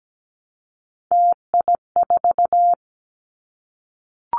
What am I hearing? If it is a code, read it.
Morse code, heard as a keyed tone, about 17 wpm, 700 Hz. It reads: TI4
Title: Callsign Prefixes for Top 100 DXCC - 17wpm